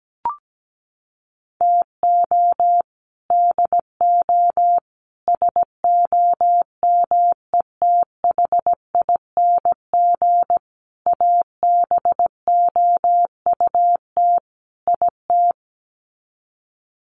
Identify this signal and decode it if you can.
Morse code, heard as a keyed tone, about 17 wpm, 700 Hz. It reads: TO DO SOMETHING ABOUT IT
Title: Sets of 5 Words 17wpm